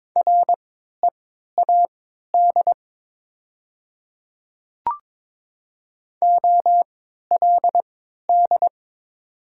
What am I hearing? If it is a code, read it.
Morse code, heard as a keyed tone, about 21 wpm, spaced as for 12 wpm, 700 Hz. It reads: READ OLD
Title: 3x Character Spacing - Top 200 Words - 22wpm